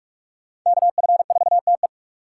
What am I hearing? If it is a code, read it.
Morse code, heard as a keyed tone, about 46 wpm, 700 Hz. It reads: KF4TE
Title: USA Call Signs - 5 Characters - 3x Character Spacing - 45wpm